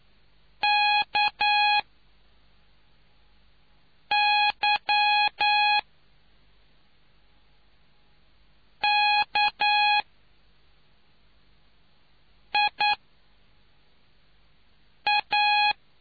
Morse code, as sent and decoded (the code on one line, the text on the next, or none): -.- -.-- -.- .. .-
KYKIA